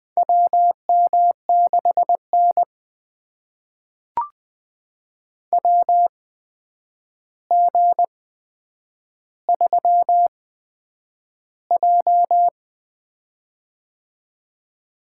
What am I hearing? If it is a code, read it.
Morse code, heard as a keyed tone, about 20 wpm, spaced as for 9 wpm, 700 Hz. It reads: WM6N WG3J
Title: USA Call Signs - 8x Character Spacing - 20wpm